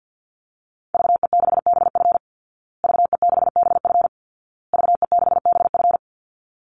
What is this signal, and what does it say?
Morse code, heard as a keyed tone, about 50 wpm, 700 Hz. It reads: VE6BF VE6BF VE6BF
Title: Call Signs Repeated 2x - 50wpm